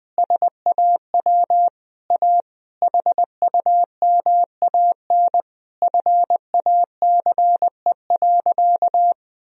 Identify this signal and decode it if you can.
Morse code, heard as a keyed tone, about 20 wpm, 700 Hz. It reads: SAW A HUMAN FACE.
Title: Sentences from Top 1000 Words - Review Part 3 - 20wpm